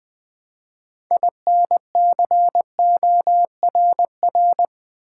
Morse code, heard as a keyed tone, about 20 wpm, 700 Hz. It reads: INCORR
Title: Build A Word - 3 Letters At A Time - 9-Letter Words - 20wpm